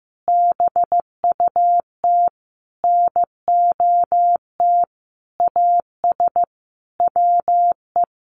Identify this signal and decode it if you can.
Morse code, heard as a keyed tone, about 15 wpm, 700 Hz. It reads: BUT NOT AS WE
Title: Sentences from Top 1100 Words 15wpm